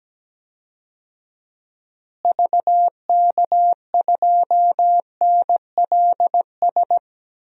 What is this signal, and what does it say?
Morse code, heard as a keyed tone, about 17 wpm, 700 Hz. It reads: VK2NLS
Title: DXCC-100 Call Signs - 6 Characters - Repeated 3x - 17wpm